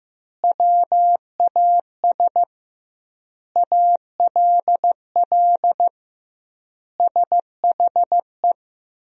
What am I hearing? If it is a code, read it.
Morse code, heard as a keyed tone, about 15 wpm, 700 Hz. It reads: WAS ALL SHE
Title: 2x Word-Spacing - Sentences from Top 100 Words - 15wpm